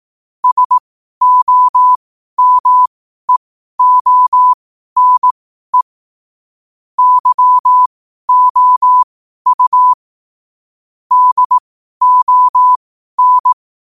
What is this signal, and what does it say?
Morse code, heard as a keyed tone, about 18 wpm, spaced as for 12 wpm, 1000 Hz. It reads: SOMEONE YOU DON
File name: qotdfiles / 12wpm